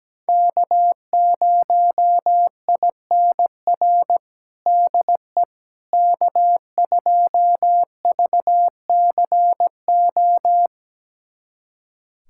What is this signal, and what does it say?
Morse code, heard as a keyed tone, about 17 wpm, 700 Hz. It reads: K0INR DE K2VCO
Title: USA Callsign DE Callsign - 5 Characters - 17wpm